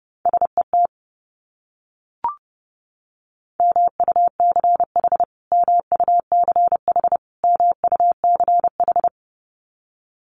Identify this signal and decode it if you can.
Morse code, heard as a keyed tone, about 30 wpm, 700 Hz. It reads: SET MUCH MUCH MUCH
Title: Top 200 Words Repeated 30wpm